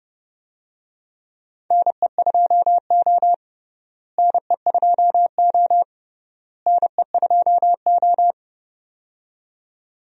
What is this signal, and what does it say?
Morse code, heard as a keyed tone, about 32 wpm, 700 Hz. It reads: NE2O NE2O NE2O
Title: USA Call Signs - 4 Characters - Repeated 3x - 30wpm